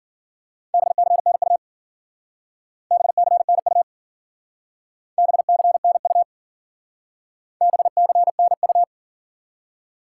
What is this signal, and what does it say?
Morse code, heard as a keyed tone, about 50 wpm, 700 Hz. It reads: BCNU BCNU BCNU BCNU
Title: Speed-Racing Top 90 QSO Elements 40wpm